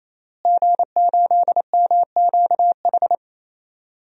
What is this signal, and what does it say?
Morse code, heard as a keyed tone, about 27 wpm, 700 Hz. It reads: G8MQH
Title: DXCC-100 Call Signs - 5 Characters - Repeated 3x - 28wpm